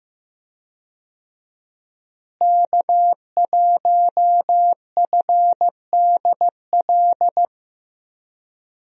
Morse code, heard as a keyed tone, about 15 wpm, 700 Hz. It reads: K1FDL
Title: USA Call Signs - 5 Characters - Repeated 2x - 15wpm